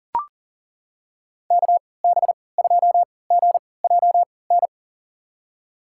Morse code, heard as a keyed tone, about 40 wpm, spaced as for 22 wpm, 700 Hz. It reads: KB2GJN
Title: USA Call Signs - 6 Characters - 3x Character Spacing - 40wpm